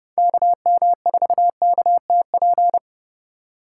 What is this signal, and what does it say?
Morse code, heard as a keyed tone, about 29 wpm, 700 Hz. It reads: KM4KTP